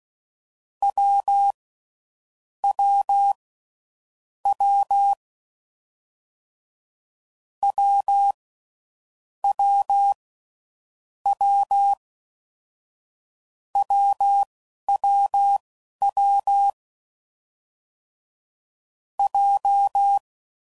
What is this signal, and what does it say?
Morse code, heard as a keyed tone, about 16 wpm, 800 Hz. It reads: W W W W W W W W W J
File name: Combinations / Difficult Combos / AWJ1